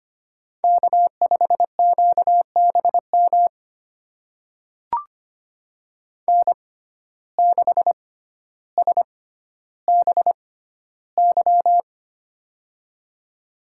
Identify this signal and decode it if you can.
Morse code, heard as a keyed tone, about 25 wpm, spaced as for 13 wpm, 690 Hz. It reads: K5QBM N6SBY
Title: USA Call Signs - 5 Characters - 6x Character Spacing - 25wpm